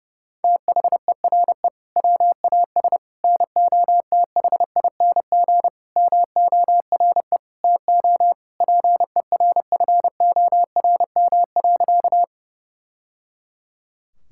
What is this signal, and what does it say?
Morse code, heard as a keyed tone, about 32 wpm, 700 Hz. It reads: THERE WAS NOTHING MORE TO PERFORM.
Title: Sentences from Top 1000 Words - Review Part 5 - 30wpm